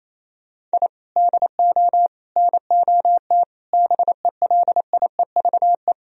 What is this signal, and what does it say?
Morse code, heard as a keyed tone, about 34 wpm, 700 Hz. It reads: I DO NOT BELIEVE
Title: Speed-Racing - Sets of 5 Words 28wpm